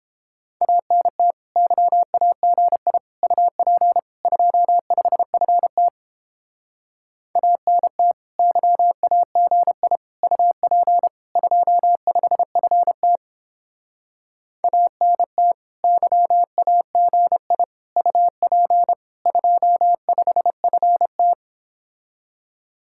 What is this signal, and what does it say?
Morse code, heard as a keyed tone, about 31 wpm, 700 Hz. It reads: ANT YAGI UP 25FT ANT YAGI UP 25FT ANT YAGI UP 25FT